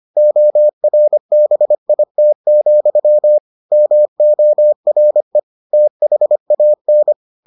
Morse code, heard as a keyed tone, about 25 wpm, 600 Hz. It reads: ORBIT, MORE THAN